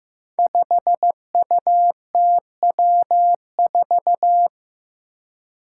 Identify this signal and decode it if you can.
Morse code, heard as a keyed tone, about 15 wpm, 700 Hz. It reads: 5UTW4
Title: Five Challenging Letter-Number Combinations 15wpm